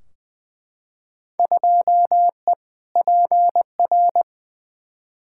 Morse code, heard as a keyed tone, about 20 wpm, 700 Hz. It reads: 2E PR
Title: Field Day Exchanges Encore - 20wpm